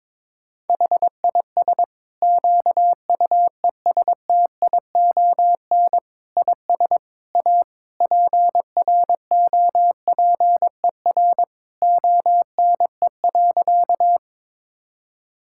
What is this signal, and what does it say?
Morse code, heard as a keyed tone, about 22 wpm, 700 Hz. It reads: HIS QUESTION IS A PROPER ONE.
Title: Sentences from Top 1000 Words - Review Part 5 - 22wpm